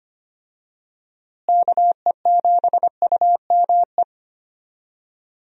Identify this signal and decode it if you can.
Morse code, heard as a keyed tone, about 25 wpm, 700 Hz. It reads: KE7UME
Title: USA Call Signs - 6 Characters - Repeated 2x - 25wpm